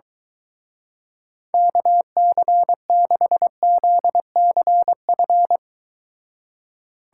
Morse code, heard as a keyed tone, about 22 wpm, 700 Hz. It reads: KC6ZCF